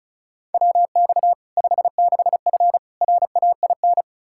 Morse code, heard as a keyed tone, about 36 wpm, 700 Hz. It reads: WX 56F RAIN
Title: Practice QSO - Exchange 2 - Info - 35wpm